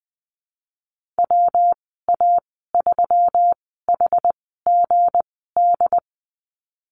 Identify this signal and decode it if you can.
Morse code, heard as a keyed tone, about 19 wpm, spaced as for 14 wpm, 700 Hz. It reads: WA3HGD
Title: 179 Lesson 29g - Slash - Call Signs 20wpm